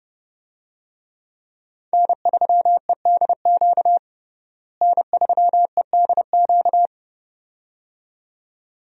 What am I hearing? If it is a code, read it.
Morse code, heard as a keyed tone, about 30 wpm, 700 Hz. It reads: N3EDQ N3EDQ
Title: USA Call Signs - 5 Characters - Repeated 2x - 30wpm